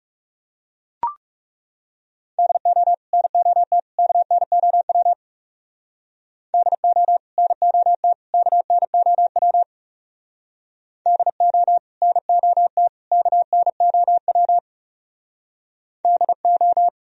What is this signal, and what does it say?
Morse code, heard as a keyed tone, about 37 wpm, 700 Hz. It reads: DO NOT KNOW DO NOT KNOW DO NOT KNOW DO
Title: Speed-Racing - Sets of 3 Words 30wpm